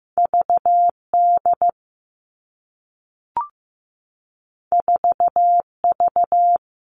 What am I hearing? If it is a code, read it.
Morse code, heard as a keyed tone, about 15 wpm, 700 Hz. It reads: VD 4V
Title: All Cut Numbers - 2-Digit 15wpm